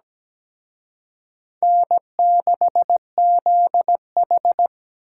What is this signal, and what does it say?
Morse code, heard as a keyed tone, about 20 wpm, 700 Hz. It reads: N6ZH
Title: Speed Racing - USA 1x2 Call Signs 17wpm